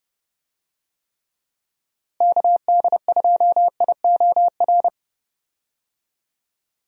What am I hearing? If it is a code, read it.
Morse code, heard as a keyed tone, about 30 wpm, 700 Hz. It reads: KD2IOR
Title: USA Call Signs - 6 Characters - Repeated 2x - 30wpm